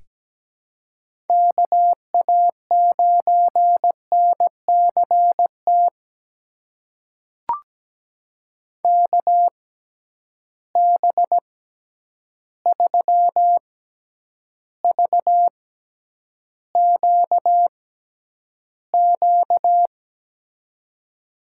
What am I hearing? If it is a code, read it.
Morse code, heard as a keyed tone, about 17 wpm, spaced as for 9 wpm, 700 Hz. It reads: KA9NCT KB3VQQ
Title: USA Call Signs - 6 Characters - 6x Character Spacing - 17wpm